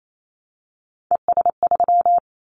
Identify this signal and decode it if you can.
Morse code, heard as a keyed tone, about 28 wpm, 700 Hz. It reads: ES3
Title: Callsign Prefixes for Top 50 DXCC - 28wpm